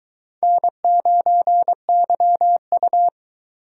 Morse code, heard as a keyed tone, about 22 wpm, 700 Hz. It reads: N9YU